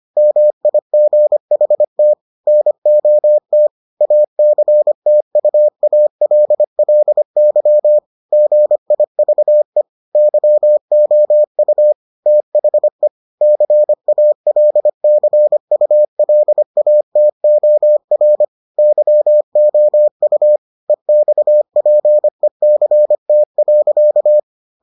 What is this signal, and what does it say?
Morse code, heard as a keyed tone, about 25 wpm, 600 Hz. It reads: MIGHT NOT ACTUALLY GIVE YOU THE CALCULATOR YOU EXPECT.